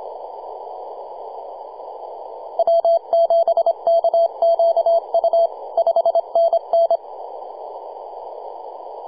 Morse code, heard as a keyed tone, about 25 wpm, 670 Hz. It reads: W7KQU 5NN
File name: posts / 2015-sept-26-2040z-v73a-21-025-cw